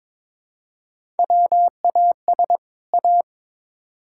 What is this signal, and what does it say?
Morse code, heard as a keyed tone, about 22 wpm, 700 Hz. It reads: WAS A
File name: Build A Phrase - 2-Word Phrases - 22wpm